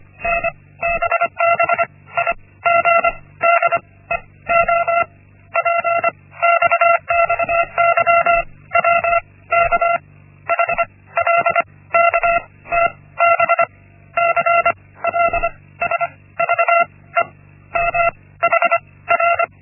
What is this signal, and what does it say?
Morse code, heard as a keyed tone, about 25 wpm, spaced as for 16 wpm, 680 Hz. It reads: NBBIG DEOPX XYWKH LKTBC LSVEM HR
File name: Linksmakalnis-1